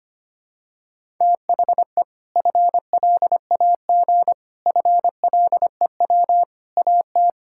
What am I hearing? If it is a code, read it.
Morse code, heard as a keyed tone, about 25 wpm, 700 Hz. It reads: THE FLAG FLEW AT
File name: practice / 25wpm / Sentences with 4-Word Phrases - 25wpm